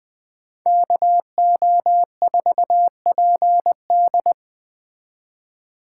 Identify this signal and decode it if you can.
Morse code, heard as a keyed tone, about 20 wpm, 700 Hz. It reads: KO4PD
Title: Speed Racing - USA 2x2 Call Signs 17wpm